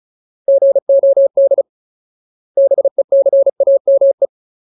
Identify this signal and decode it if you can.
Morse code, heard as a keyed tone, about 34 wpm, 550 Hz. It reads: GOD BECAME
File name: Ultimate Word List / 35WPM Ultimate